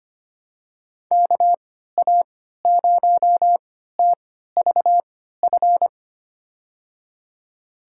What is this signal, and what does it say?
Morse code, heard as a keyed tone, about 25 wpm, spaced as for 14 wpm, 700 Hz. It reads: KA0TVF
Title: USA Call Signs - 6 Characters - 3x Character Spacing - 25wpm